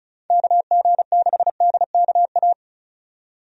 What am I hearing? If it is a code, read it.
Morse code, heard as a keyed tone, about 36 wpm, 700 Hz. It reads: KG6DKA
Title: USA Call Signs - 6 Characters - 35wpm